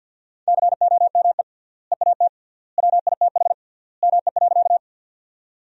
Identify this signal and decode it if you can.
Morse code, heard as a keyed tone, about 50 wpm, 700 Hz. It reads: COME EAT WITH ME.
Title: 3x Word-Spacing - Sentences from Top 300 Words - 50wpm